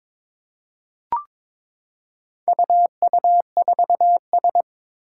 Morse code, heard as a keyed tone, about 22 wpm, 700 Hz. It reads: UU4S
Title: Four Challenging Letter-Number Combinations 22wpm